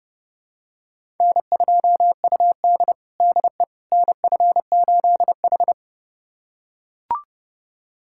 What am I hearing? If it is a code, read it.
Morse code, heard as a keyed tone, about 30 wpm, 700 Hz. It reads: N2UD DE NF8H
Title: USA Callsign DE Callsign - 4 Characters - 30wpm